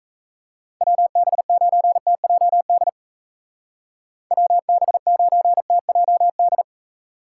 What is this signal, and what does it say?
Morse code, heard as a keyed tone, about 34 wpm, 700 Hz. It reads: WB9TJD WB9TJD
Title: Speed Racing - USA Call Signs 28wpm